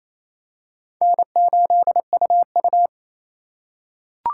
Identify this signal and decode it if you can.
Morse code, heard as a keyed tone, about 28 wpm, 700 Hz. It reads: N8UU
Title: USA Call Signs - 4 Characters - 4x Character Spacing - 28wpm